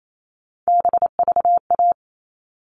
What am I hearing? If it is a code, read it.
Morse code, heard as a keyed tone, about 28 wpm, 700 Hz. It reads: BVA
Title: All Cut Numbers - 3-Digit 28wpm